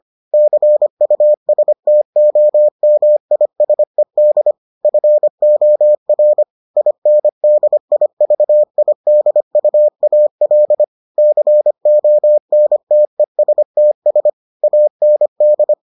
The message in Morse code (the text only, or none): CUSTOMISED FOR INDIVIDUAL CONTESTS AND